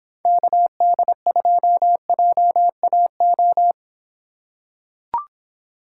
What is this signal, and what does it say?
Morse code, heard as a keyed tone, about 24 wpm, 700 Hz. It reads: KD2JAO